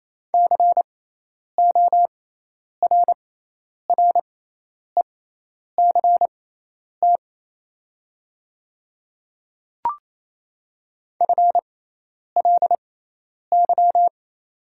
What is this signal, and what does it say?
Morse code, heard as a keyed tone, about 28 wpm, spaced as for 10 wpm, 700 Hz. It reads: CORRECT FLY
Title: 6x Character Spacing - Top 500 Words - 28wpm